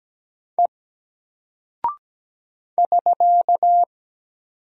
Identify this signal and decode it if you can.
Morse code, heard as a keyed tone, about 17 wpm, 700 Hz. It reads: E <SK>
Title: Single Character (Letter:Number:Punctuation:Prosign) Alt 17wpm